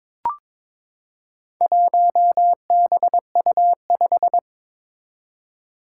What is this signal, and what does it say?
Morse code, heard as a keyed tone, about 22 wpm, 700 Hz. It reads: 1BU5
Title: Four Challenging Letter-Number Combinations 22wpm